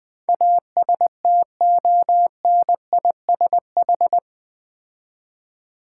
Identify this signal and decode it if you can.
Morse code, heard as a keyed tone, about 20 wpm, 700 Hz. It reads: ASTONISH
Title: Build A Word - 4 Letters At A Time - 11-Letter Words - 20wpm